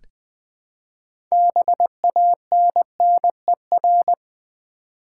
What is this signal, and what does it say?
Morse code, heard as a keyed tone, about 20 wpm, 700 Hz. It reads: BANNER
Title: Build A Word - 3 Letters At A Time - 6-Letter Words - 20wpm